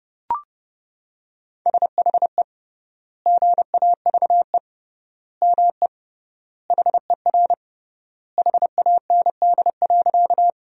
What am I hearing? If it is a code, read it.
Morse code, heard as a keyed tone, about 30 wpm, 700 Hz. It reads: SHE GAVE ME HER HAND.
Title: 3x Word-Spacing - Sentences from Top 500 Words - 30wpm